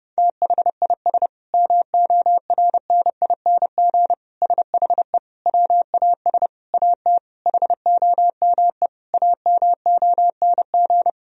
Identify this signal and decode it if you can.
Morse code, heard as a keyed tone, about 30 wpm, 700 Hz. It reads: THIS MORNING SHE WAS AT HOME AMONG
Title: Sentences from Top 1000 Words - Review Part 3 - 30wpm